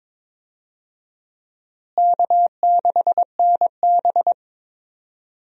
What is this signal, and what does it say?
Morse code, heard as a keyed tone, about 22 wpm, 700 Hz. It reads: K6NB
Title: USA Call Signs - 4 Characters - Repeated 2x - 22wpm